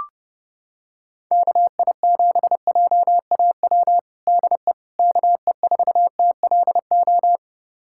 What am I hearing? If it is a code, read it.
Morse code, heard as a keyed tone, about 30 wpm, 700 Hz. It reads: KI7JAW DE KE4TLO
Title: USA Callsign DE Callsign 30wpm